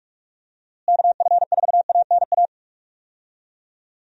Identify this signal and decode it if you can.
Morse code, heard as a keyed tone, about 41 wpm, 700 Hz. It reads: KF4ANA
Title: USA Call Signs - 6 Characters - Repeated 3x - 45wpm